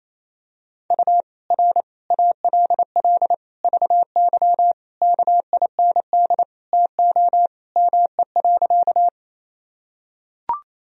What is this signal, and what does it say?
Morse code, heard as a keyed tone, about 28 wpm, 700 Hz. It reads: U R ALL VY KIND TO ME.
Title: Sentences from Top 200 Words and Abbreviations 28wpm